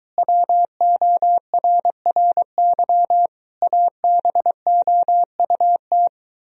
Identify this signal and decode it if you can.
Morse code, heard as a keyed tone, about 23 wpm, 700 Hz. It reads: WORRY ABOUT